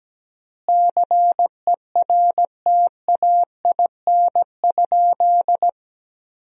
Morse code, heard as a keyed tone, about 17 wpm, 700 Hz. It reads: CERTAIN?
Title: 4x Word-Spacing - Sentences from Top 500 Words - 17wpm